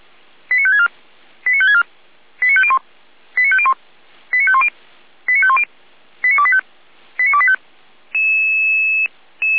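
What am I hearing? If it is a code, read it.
Five-tone selective call (ZVEI), heard as quick runs of five tones, tones of about 70 ms. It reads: 89564 89564 89501 89501 89510 89510 89285 89285
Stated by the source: ZVEI five-tone selective call 89001, received by SDR, from the Feuerwehr Orlamünde server.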